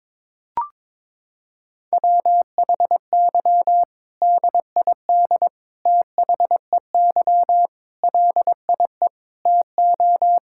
Morse code, heard as a keyed tone, about 22 wpm, 700 Hz. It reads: WHY DID THEY LIE TO
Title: Sentences from Top 1600 Words 22wpm